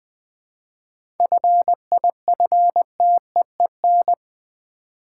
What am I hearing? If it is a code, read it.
Morse code, heard as a keyed tone, about 20 wpm, 700 Hz. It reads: FIFTEEN
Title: 7 Letter Words - First 3 Letters Given - 20wpm